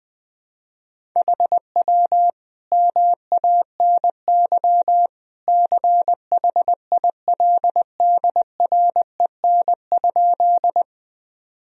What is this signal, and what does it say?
Morse code, heard as a keyed tone, about 20 wpm, 700 Hz. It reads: HW MANY CHILDREN?
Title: Sentences from Top 300 Words and Abbreviations 20wpm